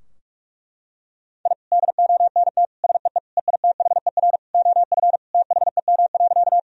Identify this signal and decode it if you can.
Morse code, heard as a keyed tone, about 46 wpm, 700 Hz. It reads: I DONT SEE EITHER OF THEM.
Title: Sentences from Top 1000 Words - Review Part 4 - 45wpm